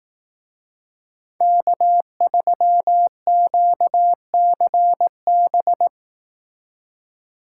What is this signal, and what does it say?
Morse code, heard as a keyed tone, about 19 wpm, 700 Hz. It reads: K3QCB